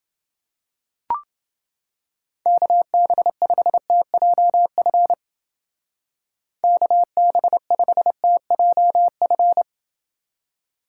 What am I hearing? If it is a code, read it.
Morse code, heard as a keyed tone, about 28 wpm, 700 Hz. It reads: KB5TJF KB5TJF